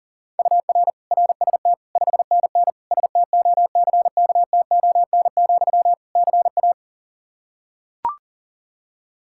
Morse code, heard as a keyed tone, about 40 wpm, 700 Hz. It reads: UR RST 5NN STOCKTON, CA
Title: K1USN SST QSO Practice 40wpm